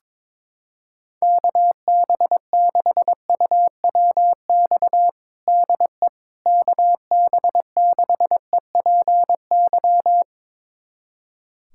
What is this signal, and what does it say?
Morse code, heard as a keyed tone, about 22 wpm, 700 Hz. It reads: KB6UWX DE KB6EPY
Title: USA Callsign DE Callsign 22wpm